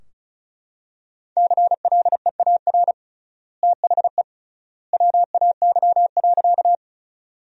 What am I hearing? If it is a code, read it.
Morse code, heard as a keyed tone, about 36 wpm, 700 Hz. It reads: CLEAR THE WAY.
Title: 3x Word-Spacing - Sentences from Top 500 Words - 35wpm